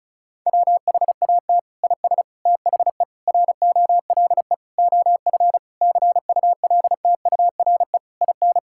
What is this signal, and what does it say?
Morse code, heard as a keyed tone, about 34 wpm, 700 Hz. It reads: WHAT IS THE ROLE OF CULTURE IN
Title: Sentences from Top 1000 Words - Review Part 6 - 35wpm